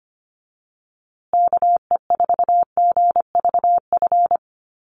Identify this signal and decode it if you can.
Morse code, heard as a keyed tone, about 25 wpm, 700 Hz. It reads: KE4GVF
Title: USA Call Signs Repeated 3x 25wpm